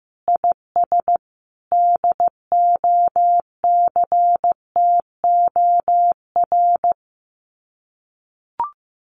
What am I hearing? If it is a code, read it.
Morse code, heard as a keyed tone, about 15 wpm, 700 Hz. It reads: IS DOCTOR
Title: Common QSO Phrases Encore 15wpm